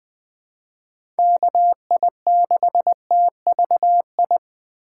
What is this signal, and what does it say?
Morse code, heard as a keyed tone, about 20 wpm, 700 Hz. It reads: KI6TVI
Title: USA Call Signs - 6 Characters - 4x Character Spacing - 20wpm